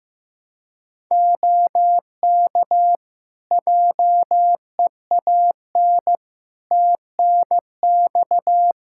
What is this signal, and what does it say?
Morse code, heard as a keyed tone, about 15 wpm, 700 Hz. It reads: OK JEAN TNX
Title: Practice QSO - Exchange 4 - Ending - 15wpm